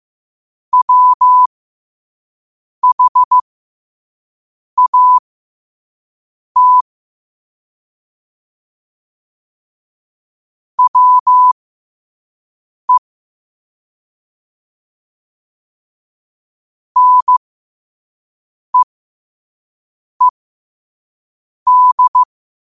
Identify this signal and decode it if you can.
Morse code, heard as a keyed tone, about 15 wpm, spaced as for 5 wpm, 1000 Hz. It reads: WHAT WE NEED
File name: qotdfiles / 5wpm